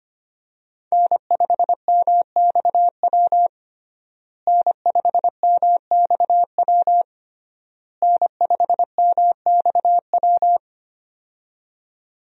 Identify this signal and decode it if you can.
Morse code, heard as a keyed tone, about 25 wpm, 700 Hz. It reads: N5MXW N5MXW N5MXW
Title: USA Call Signs - 5 Characters - Repeated 3x - 25wpm